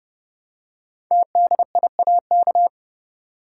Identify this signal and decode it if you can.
Morse code, heard as a keyed tone, about 29 wpm, 700 Hz. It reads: TDIAK
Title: Code Groups - 5 Letters - 30wpm